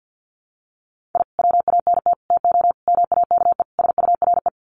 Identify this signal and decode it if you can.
Morse code, heard as a keyed tone, about 49 wpm, 700 Hz. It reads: I WANT TO MAKE SURE
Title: Sets of 5 Words Encore2 50wpm